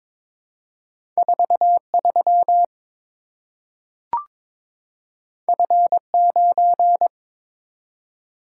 Morse code, heard as a keyed tone, about 22 wpm, 700 Hz. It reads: 43 F9
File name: practice / 22wpm / Two Letter-Number Combinations - 22wpm